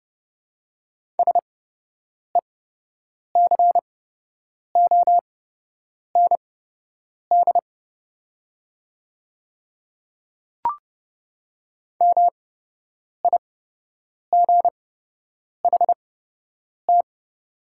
Morse code, heard as a keyed tone, about 30 wpm, spaced as for 8 wpm, 700 Hz. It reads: SECOND MIGHT
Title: 8x Character Spacing - Top 300 Words - 30wpm